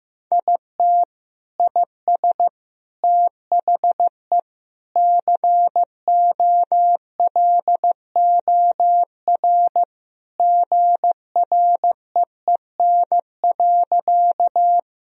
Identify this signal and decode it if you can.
Morse code, heard as a keyed tone, about 16 wpm, 700 Hz. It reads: IT IS THE COLOR GREEN.
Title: Sentences from Top 1000 Words - Review Part 2 - 15wpm